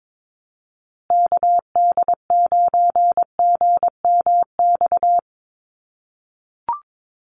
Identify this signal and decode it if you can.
Morse code, heard as a keyed tone, about 22 wpm, 700 Hz. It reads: KD9GMX
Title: USA Call Signs Repeated 3x 22wpm